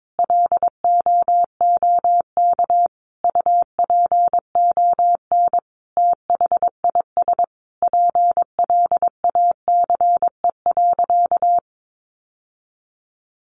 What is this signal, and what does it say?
Morse code, heard as a keyed tone, about 22 wpm, 700 Hz. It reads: LOOK UPON THIS PLACE.
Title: Sentences from Top 1100 Words 22wpm